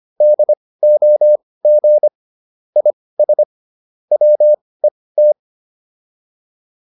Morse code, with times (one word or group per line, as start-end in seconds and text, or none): 0.2-2.1 DOG
2.8-3.4 IS
4.1-5.3 WET